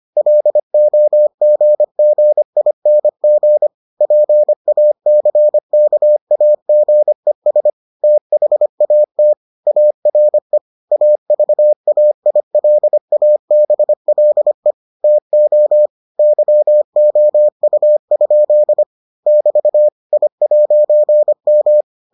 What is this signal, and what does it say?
Morse code, heard as a keyed tone, about 25 wpm, 600 Hz. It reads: LOGGING PACKAGES THAT ARE AVAILABLE TO YOU? <BT> I'M